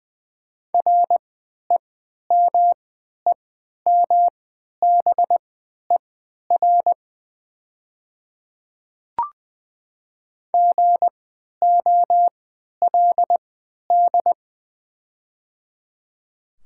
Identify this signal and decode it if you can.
Morse code, heard as a keyed tone, about 20 wpm, spaced as for 11 wpm, 700 Hz. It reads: REMEMBER GOLD
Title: 3x Character Spacing - Top 500 Words - 20wpm